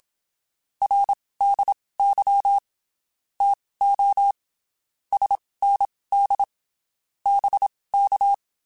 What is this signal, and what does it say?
Morse code, heard as a keyed tone, about 27 wpm, spaced as for 19 wpm, 800 Hz. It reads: RDY TO SND BK
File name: (spaced) / QSOs Spaced-mp3s / WA7QOX 1 @25 sp